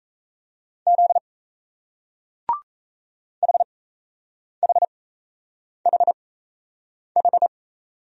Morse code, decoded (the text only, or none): Z HHHH